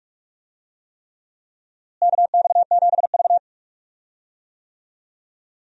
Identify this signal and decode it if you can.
Morse code, heard as a keyed tone, about 46 wpm, 700 Hz. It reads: KX7V
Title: USA Call Signs - 4 Characters - Repeated 3x - 45wpm